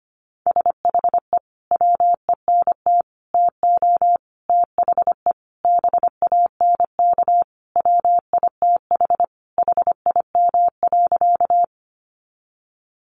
Morse code, heard as a keyed tone, about 25 wpm, 700 Hz. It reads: SHE WENT TO THE BANK WITH HIM.
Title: Sentences from Top 900 Words 25wpm